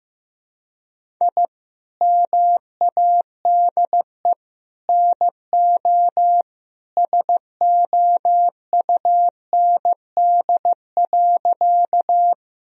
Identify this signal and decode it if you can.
Morse code, heard as a keyed tone, about 16 wpm, 700 Hz. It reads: I MADE NO SOUND.
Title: Sentences from Top 1000 Words - Review Part 1 - 15wpm